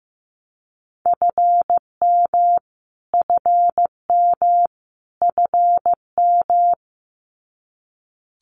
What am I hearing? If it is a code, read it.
Morse code, heard as a keyed tone, about 15 wpm, 700 Hz. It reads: FM FM FM
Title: Abbreviations Repeated 3x - 15wpm